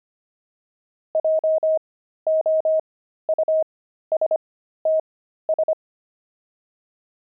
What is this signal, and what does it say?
Morse code, heard as a keyed tone, about 25 wpm, spaced as for 13 wpm, 630 Hz. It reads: JOUSTS
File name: pangram-fs-13-cs-25-crazy